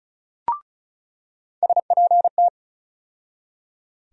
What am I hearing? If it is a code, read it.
Morse code, heard as a keyed tone, about 36 wpm, 700 Hz. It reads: SPT
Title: USA Call Sign Postfixes - 3 Letters 35wpm